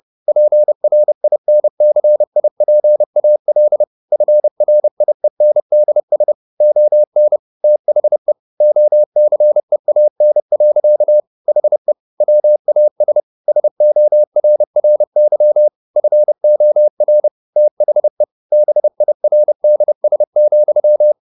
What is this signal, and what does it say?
Morse code, heard as a keyed tone, about 30 wpm, 600 Hz. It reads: PRINCIPAL FRIENDS ON THE OCEAN. HE WAS SORRY FOR THE BIRDS,
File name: the old man and the sea / the old man and the sea 1 30wpm